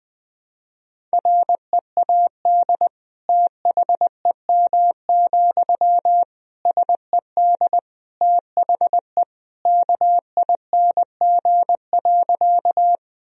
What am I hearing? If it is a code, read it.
Morse code, heard as a keyed tone, about 21 wpm, 700 Hz. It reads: READ THEM, SED THE KING.
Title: Sentences from Top 400 Words and Abbreviations 20wpm